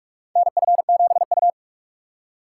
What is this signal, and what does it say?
Morse code, heard as a keyed tone, about 44 wpm, 700 Hz. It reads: NF7U
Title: USA Call Signs - 3x Character Spacing - 45wpm